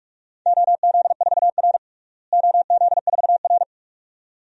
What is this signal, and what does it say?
Morse code, heard as a keyed tone, about 46 wpm, 700 Hz. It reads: OZ4R OZ4R
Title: DXCC-100 Call Signs - 4 Characters - Repeated 2x - 45wpm